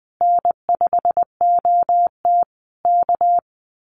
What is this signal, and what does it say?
Morse code, heard as a keyed tone, about 19 wpm, 700 Hz. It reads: N5OT K
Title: Common QSO Phrases Encore 20wpm